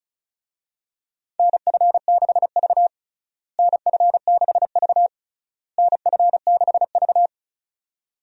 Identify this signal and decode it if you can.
Morse code, heard as a keyed tone, about 34 wpm, 700 Hz. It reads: NF6V NF6V NF6V
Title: USA Call Signs - 4 Characters - Repeated 3x - 35wpm